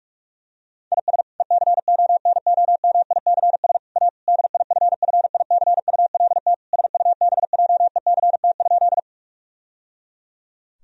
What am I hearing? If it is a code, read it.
Morse code, heard as a keyed tone, about 46 wpm, 700 Hz. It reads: IS ECONOMICS A DIFFICULT SUBJECT?
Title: Sentences from Top 1000 Words - Review Part 3 - 45wpm